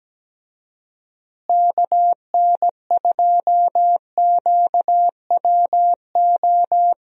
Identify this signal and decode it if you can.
Morse code, heard as a keyed tone, about 17 wpm, 700 Hz. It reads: KN2QWO
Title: USA Call Signs - 6 Characters - Repeated 2x - 17wpm